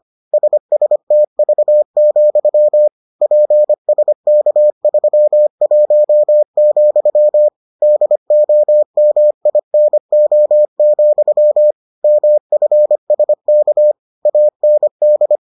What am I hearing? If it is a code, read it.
Morse code, heard as a keyed tone, about 25 wpm, 600 Hz. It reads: SSTV, PSK31, DOMINO, MFSK AND